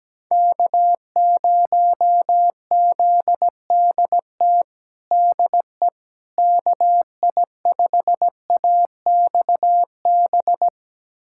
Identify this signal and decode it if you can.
Morse code, heard as a keyed tone, about 17 wpm, 700 Hz. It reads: K0ZDT DE KI5AXB
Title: USA Callsign DE Callsign 17wpm